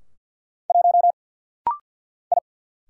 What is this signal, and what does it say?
Morse code, heard as a keyed tone, about 50 wpm, 690 Hz. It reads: 1 I
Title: Single Letter-Number Alt - Rapid-Fire 50wpm